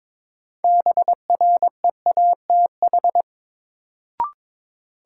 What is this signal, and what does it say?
Morse code, heard as a keyed tone, about 22 wpm, 700 Hz. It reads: BREATH
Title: Sets of 5 Words Encore - New Word Drill - 22wpm